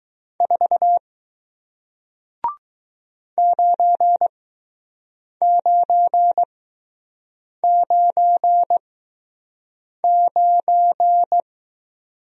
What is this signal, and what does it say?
Morse code, heard as a keyed tone, about 20 wpm, spaced as for 6 wpm, 700 Hz. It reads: 4 9999